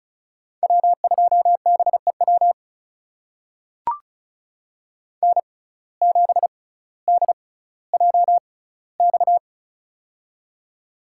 Morse code, heard as a keyed tone, about 36 wpm, spaced as for 18 wpm, 700 Hz. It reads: W2BEW N7DJX
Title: USA Call Signs - 5 Characters - 6x Character Spacing - 35wpm